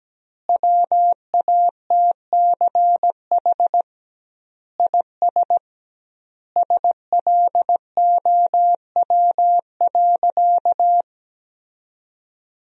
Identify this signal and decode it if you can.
Morse code, heard as a keyed tone, about 17 wpm, 700 Hz. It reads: WATCH IS SLOW.
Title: 2x Word-Spacing - Sentences from Top 400 Words - 17wpm